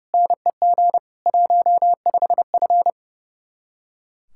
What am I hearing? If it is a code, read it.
Morse code, heard as a keyed tone, about 37 wpm, 700 Hz. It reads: NEG 15F
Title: Speed-Racing Common QSO Phrases - 30wpm